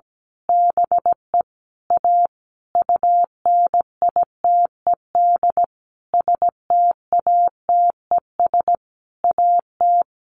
Morse code, heard as a keyed tone, about 17 wpm, 700 Hz. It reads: BE A UNITED STATES AT
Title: A Sound Of Thunder 17wpm